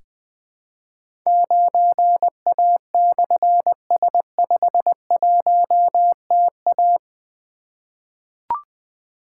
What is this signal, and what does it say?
Morse code, heard as a keyed tone, about 20 wpm, 700 Hz. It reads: 9A/S51TA
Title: Difficult Call Signs 20wpm